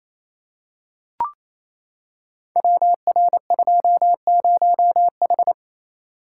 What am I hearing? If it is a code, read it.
Morse code, heard as a keyed tone, about 29 wpm, 700 Hz. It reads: WR20H
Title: Code Groups - 5 Letter:Numbers - 28wpm